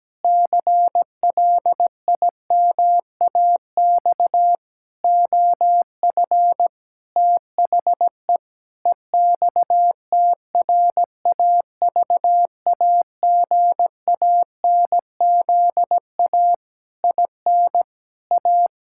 Morse code, heard as a keyed tone, about 17 wpm, 700 Hz. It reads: CLIMAX OF THE EXTRAVAGANZA IN A
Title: Pangram Challenge v2 - 17wpm